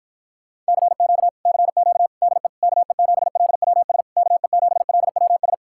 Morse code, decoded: CQ CQ DE KE7LWS KE7LWS

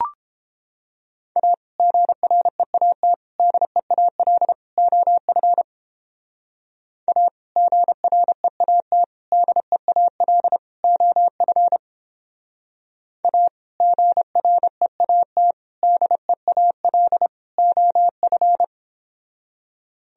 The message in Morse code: A GREAT DEAL OF A GREAT DEAL OF A GREAT DEAL OF

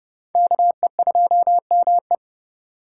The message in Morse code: KE2ME

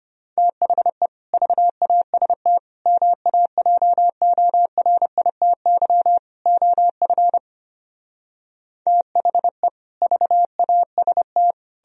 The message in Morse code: THE VAST MAJORITY OF THE VAST

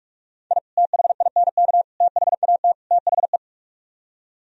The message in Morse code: I THINK THAT THE

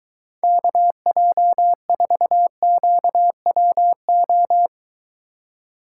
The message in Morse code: KJ4QWO

